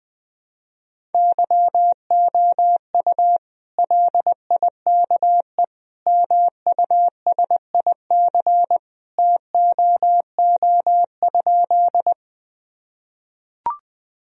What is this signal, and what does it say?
Morse code, heard as a keyed tone, about 21 wpm, 700 Hz. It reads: YOU LIKE MUSIC TOO?
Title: Sentences from Top 1000 Words - Review Part 1 - 20wpm